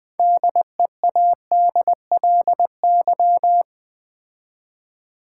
Morse code, heard as a keyed tone, about 20 wpm, 700 Hz. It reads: DEADLY
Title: 6 Letter Words - First 3 Letters Given - 20wpm